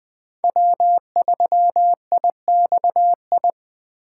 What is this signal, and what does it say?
Morse code, heard as a keyed tone, about 20 wpm, 700 Hz. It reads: W3IXI
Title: USA Call Signs - 5 Characters - 2x Character Spacing - 20wpm